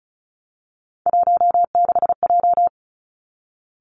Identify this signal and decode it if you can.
Morse code, heard as a keyed tone, about 34 wpm, 700 Hz. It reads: 16J
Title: Three Challenging Letter-Number Combinations 35wpm